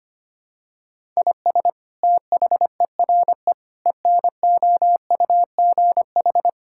Morse code, heard as a keyed tone, about 25 wpm, 700 Hz. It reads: IS THERE ENOUGH
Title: Sentences from Top 1000 Words - Review Part 5 - 25wpm